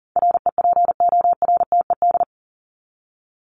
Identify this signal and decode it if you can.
Morse code, heard as a keyed tone, about 38 wpm, 700 Hz. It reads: REPORTED
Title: ED Suffix - Top 100 Words with Base Words 40wpm